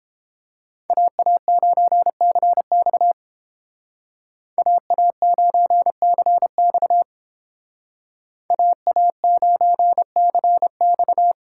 Morse code, heard as a keyed tone, about 30 wpm, 700 Hz. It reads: AA9CX AA9CX AA9CX